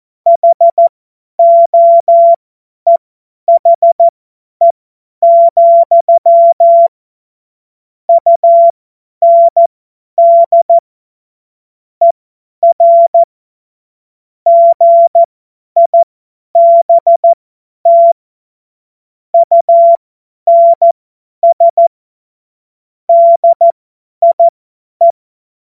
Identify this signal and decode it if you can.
Morse code, heard as a keyed tone, about 14 wpm, spaced as for 10 wpm, 680 Hz. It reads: HOEHE, UND ER GIBT UNS DIE
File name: Morsekurs / Witze / Witz 21. Juni 2022 50 F70